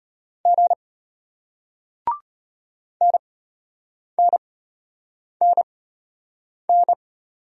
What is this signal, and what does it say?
Morse code, heard as a keyed tone, about 33 wpm, spaced as for 8 wpm, 700 Hz. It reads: G NNNN